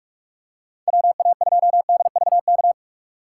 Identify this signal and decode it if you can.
Morse code, heard as a keyed tone, about 41 wpm, 700 Hz. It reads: WA2DVK
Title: USA Call Signs - 6 Characters - 2x Character Spacing - 45wpm